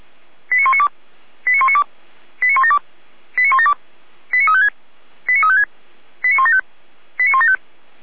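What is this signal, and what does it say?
Five-tone selective call (ZVEI), heard as quick runs of five tones, tones of about 70 ms. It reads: 89192 89192 89172 89172 89356 89356 89175 89175
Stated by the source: ZVEI five-tone selective call received on an SDR.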